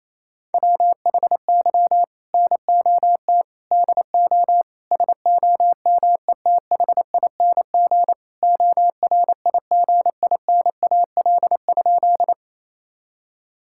Morse code, heard as a keyed tone, about 28 wpm, 700 Hz. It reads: WHY NOT DO SOMETHING ORIGINAL?
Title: Sentences from Top 1000 Words - Review Part 6 - 28wpm